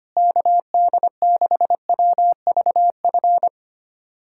Morse code, heard as a keyed tone, about 25 wpm, 700 Hz. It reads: KD6WVF
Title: USA Callsign DE Callsign - 6 Characters - 25wpm